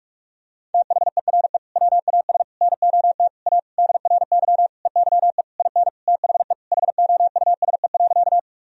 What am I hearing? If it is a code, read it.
Morse code, heard as a keyed tone, about 46 wpm, 700 Hz. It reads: THERE WAS NOT A DRY EYE IN THE HOUSE.
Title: Sentences from Top 1000 Words - Review Part 3 - 45wpm